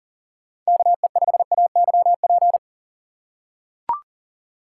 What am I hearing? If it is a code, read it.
Morse code, heard as a keyed tone, about 40 wpm, 700 Hz. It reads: KE5AYP
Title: USA Call Signs - 6 Characters - 4x Character Spacing - 40wpm